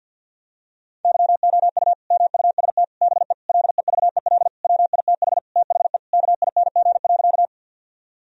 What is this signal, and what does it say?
Morse code, heard as a keyed tone, about 50 wpm, 700 Hz. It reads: YOU MUST BE LEVEL WITH THE KING.
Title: Sentences from Top 1000 Words - Review Part 4 - 50wpm